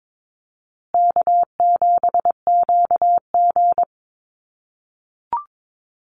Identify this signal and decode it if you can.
Morse code, heard as a keyed tone, about 22 wpm, 690 Hz. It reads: K7QG
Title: USA Call Signs Repeated 3x 22wpm